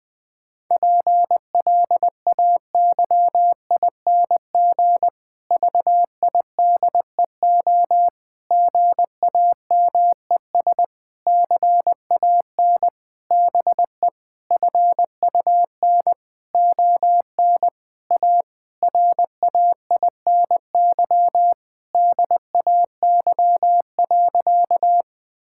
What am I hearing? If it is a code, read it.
Morse code, heard as a keyed tone, about 21 wpm, 700 Hz. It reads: PLAYING VIDEO GAMES CAN BE FUN ON A RAINY DAY.
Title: Difficult Sentences with Hints - 20wpm